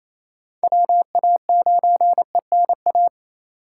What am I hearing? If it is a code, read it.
Morse code, heard as a keyed tone, about 28 wpm, 700 Hz. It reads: WA9ENA
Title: K1USN SST QSO Practice - 28wpm